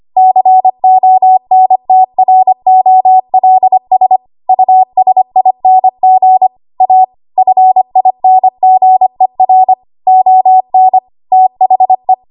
Morse code, heard as a keyed tone, about 25 wpm, 750 Hz. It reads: CONTROLS USING A FINGER ON THE
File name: Archive / 25 WPM / 25WPM